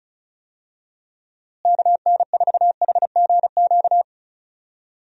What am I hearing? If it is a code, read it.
Morse code, heard as a keyed tone, about 34 wpm, 700 Hz. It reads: KN4HGQ
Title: USA Call Signs - 6 Characters - Repeated 3x - 35wpm